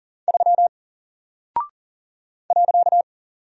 Morse code, heard as a keyed tone, about 40 wpm, 700 Hz. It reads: 3 .
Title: Single Character (Letter:Number:Punctuation:Prosign) Alt 40wpm